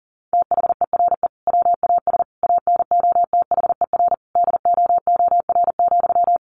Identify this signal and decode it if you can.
Morse code, heard as a keyed tone, about 40 wpm, 700 Hz. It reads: THERE WAS ANOTHER DOOR,
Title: Sentences from Top 1300 Words 40wpm